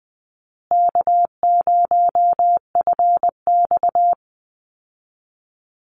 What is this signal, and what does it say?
Morse code, heard as a keyed tone, about 20 wpm, 700 Hz. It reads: K0FX
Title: Easy Call Signs Repeated 2x - 20wpm